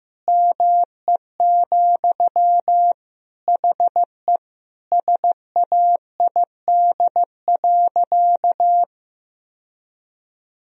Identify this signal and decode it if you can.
Morse code, heard as a keyed tone, about 16 wpm, 700 Hz. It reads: ME, HE SAID.
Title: Sentences from Top 1000 Words - Review Part 1 - 15wpm